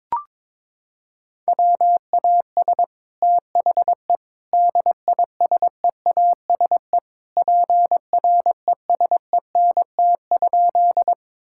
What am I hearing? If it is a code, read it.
Morse code, heard as a keyed tone, about 22 wpm, 700 Hz. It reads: WAS THE DISEASE PRESENT?
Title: Sentences from Top 1000 Words - Review Part 6 - 22wpm